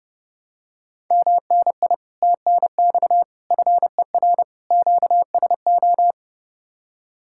Morse code, heard as a keyed tone, about 30 wpm, 700 Hz. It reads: MNI TNX FER QSO